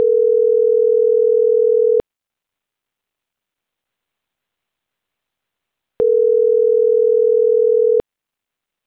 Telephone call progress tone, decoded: ringback tone

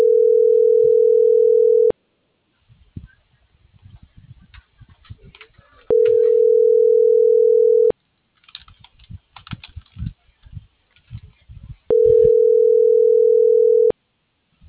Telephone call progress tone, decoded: ringback tone

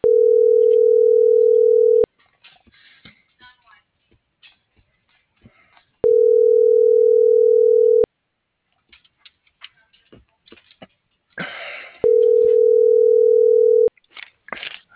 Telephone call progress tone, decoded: ringback tone

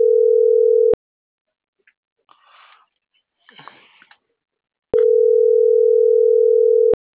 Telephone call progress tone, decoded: ringback tone

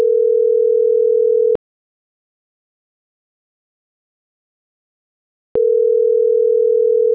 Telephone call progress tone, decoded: ringback tone

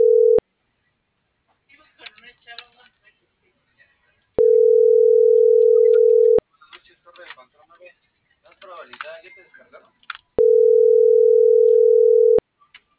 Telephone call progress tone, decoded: ringback tone

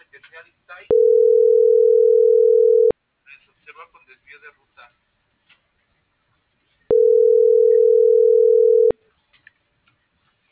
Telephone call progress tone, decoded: ringback tone

